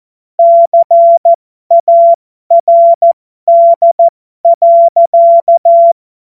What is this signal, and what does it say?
Morse code, heard as a keyed tone, about 14 wpm, spaced as for 12 wpm, 680 Hz. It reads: CARD.